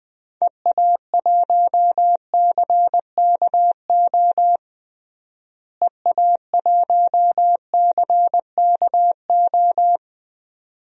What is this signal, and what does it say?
Morse code, heard as a keyed tone, about 20 wpm, 700 Hz. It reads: EA1CKO EA1CKO
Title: DXCC-100 Call Signs - 6 Characters - Repeated 2x - 20wpm